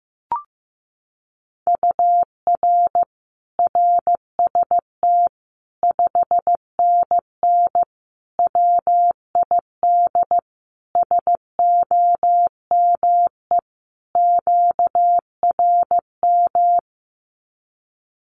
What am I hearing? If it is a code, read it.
Morse code, heard as a keyed tone, about 15 wpm, 700 Hz. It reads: UR RST 5NN WID SOME QRM
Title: Common QSO Phrases 15wpm